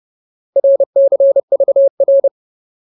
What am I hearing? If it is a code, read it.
Morse code, heard as a keyed tone, about 30 wpm, 550 Hz. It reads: RCVR
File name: Ultimate Word List / 30WPM Ultimate